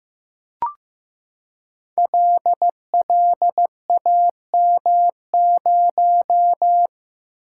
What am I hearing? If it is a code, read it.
Morse code, heard as a keyed tone, about 15 wpm, 700 Hz. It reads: LLAM0
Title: Code Groups - 5 Letter:Numbers - 15wpm